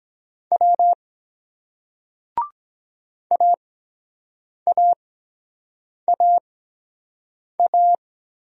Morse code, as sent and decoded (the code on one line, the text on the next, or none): .-- / .- .- .- .-
W AAAA